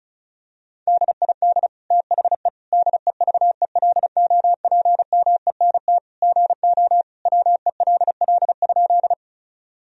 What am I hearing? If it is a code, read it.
Morse code, heard as a keyed tone, about 34 wpm, 700 Hz. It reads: DID THE DEVELOPMENT GO WELL?
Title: Sentences from Top 1000 Words - Review Part 4 - 35wpm